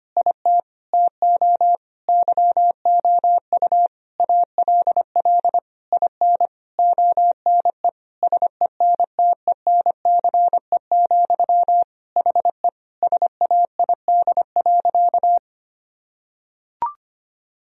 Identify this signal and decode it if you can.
Morse code, heard as a keyed tone, about 25 wpm, 700 Hz. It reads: IT TO YOU ALL IN ONE SENTENCE, HE SAID.
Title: Sentences from Top 1000 Words - Review Part 1 - 25wpm